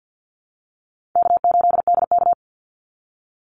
Morse code, heard as a keyed tone, about 50 wpm, 700 Hz. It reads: K8DK
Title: Call Signs - 50wpm